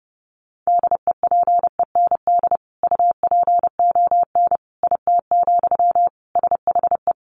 Morse code, heard as a keyed tone, about 30 wpm, 700 Hz. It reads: DEPEND UPON IT, SHE
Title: Sentences from Top 1300 Words 30wpm